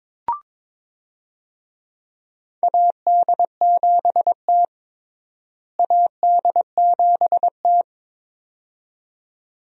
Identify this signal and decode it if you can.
Morse code, heard as a keyed tone, about 22 wpm, 700 Hz. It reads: AD7T AD7T
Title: USA Call Signs - 4 Characters - Repeated 2x - 22wpm